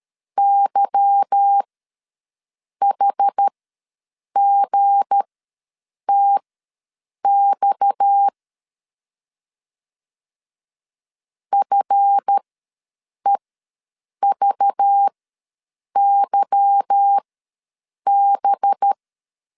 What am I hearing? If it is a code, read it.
Morse code, heard as a keyed tone, about 13 wpm, spaced as for 7 wpm, 800 Hz. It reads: YHGTX FEVYB
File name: LEZIONI / 8 YJ45 / 8 Y ws 12wpm 3409